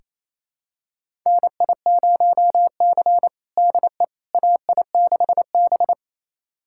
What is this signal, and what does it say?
Morse code, heard as a keyed tone, about 28 wpm, 700 Hz. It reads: NI0C DE AI6B
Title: USA Callsign DE Callsign - 4 Characters - 28wpm